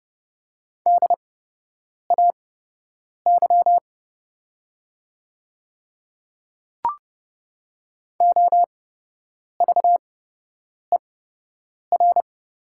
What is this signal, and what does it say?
Morse code, heard as a keyed tone, about 30 wpm, spaced as for 8 wpm, 700 Hz. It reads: DAY OVER
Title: 8x Character Spacing - Top 100 Words - 30wpm